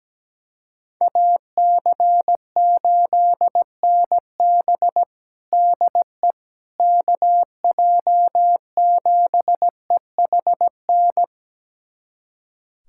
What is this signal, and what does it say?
Morse code, heard as a keyed tone, about 17 wpm, 700 Hz. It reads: AC8NB DE KJ7EHN
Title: USA Callsign DE Callsign 17wpm